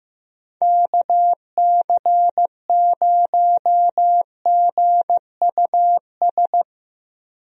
Morse code, heard as a keyed tone, about 16 wpm, 700 Hz. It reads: KC0GUS
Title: USA Call Signs - 6 Characters - Repeated 2x - 15wpm